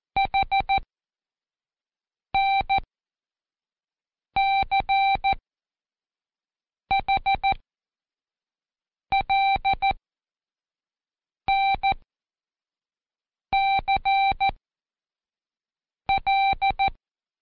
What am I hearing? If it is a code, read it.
Morse code, heard as a keyed tone, about 14 wpm, spaced as for 5 wpm, 780 Hz. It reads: HNCHLNCL
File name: Code Quick / AudioDownld1 / Seg36-Day4-Intro N